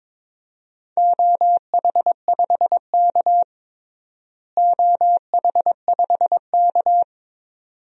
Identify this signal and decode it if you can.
Morse code, heard as a keyed tone, about 22 wpm, 700 Hz. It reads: OH5K OH5K
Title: DXCC-100 Call Signs - 4 Characters - Repeated 2x - 22wpm